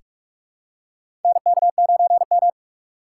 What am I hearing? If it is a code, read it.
Morse code, heard as a keyed tone, about 42 wpm, 700 Hz. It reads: NK9M
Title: USA Call Signs - 4 Characters - 4x Character Spacing - 45wpm